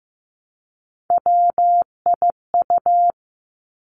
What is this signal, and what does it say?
Morse code, heard as a keyed tone, about 15 wpm, 700 Hz. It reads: WIU
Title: Three Challenging Letter-Number Combinations 15wpm